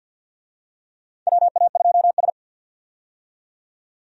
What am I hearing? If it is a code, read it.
Morse code, heard as a keyed tone, about 52 wpm, 700 Hz. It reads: WA2S
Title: USA Call Signs - 4 Characters - 50wpm